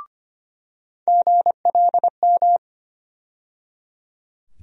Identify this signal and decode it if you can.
Morse code, heard as a keyed tone, about 25 wpm, 700 Hz. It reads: GLM
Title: USA Call Sign Postfixes - 3 Letters 25wpm